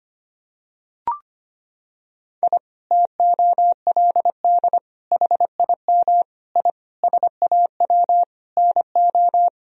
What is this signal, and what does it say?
Morse code, heard as a keyed tone, about 25 wpm, 700 Hz. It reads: I TOLD HIM I SAW NO
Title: Sentences from Top 1000 Words - Review Part 1 - 25wpm